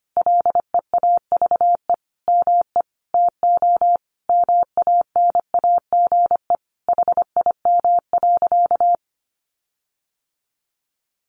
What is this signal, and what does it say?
Morse code, heard as a keyed tone, about 25 wpm, 700 Hz. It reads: LEAVE ME TO MANAGE HIM.
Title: Sentences from Top 1400 Words 25wpm